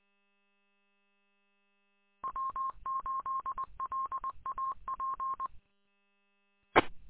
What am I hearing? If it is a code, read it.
Morse code, heard as a keyed tone, about 20 wpm, 1060 Hz. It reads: W8LAP E
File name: w8lap- -combined